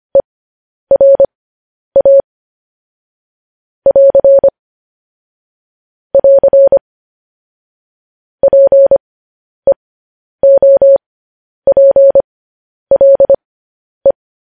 Morse code, heard as a keyed tone, about 25 wpm, spaced as for 10 wpm, 550 Hz. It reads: ERA <AR> <AR> PEOPLE
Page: news headlines in morse code at 10 wpm